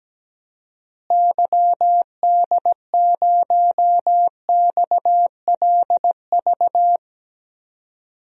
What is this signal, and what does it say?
Morse code, heard as a keyed tone, about 17 wpm, 700 Hz. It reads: YD0XLV
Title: DXCC-100 Call Signs - 6 Characters - Repeated 3x - 17wpm